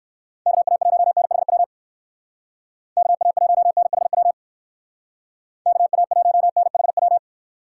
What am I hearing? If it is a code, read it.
Morse code, heard as a keyed tone, about 63 wpm, 700 Hz. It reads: KA1NHW KA1NHW KA1NHW